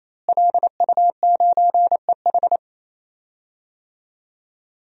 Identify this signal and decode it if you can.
Morse code, heard as a keyed tone, about 28 wpm, 700 Hz. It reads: LU9EH
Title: DXCC-100 Call Signs - 5 Characters - Repeated 3x - 28wpm